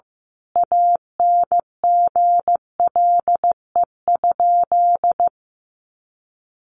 Morse code, heard as a keyed tone, about 15 wpm, 700 Hz. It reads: ANGLE?
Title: Sentences from Top 1500 Words 15wpm